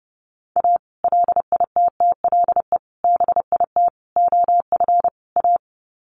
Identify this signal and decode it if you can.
Morse code, heard as a keyed tone, about 30 wpm, 700 Hz. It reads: A LITTLE BIT OF A
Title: Sets of 5 Words 30wpm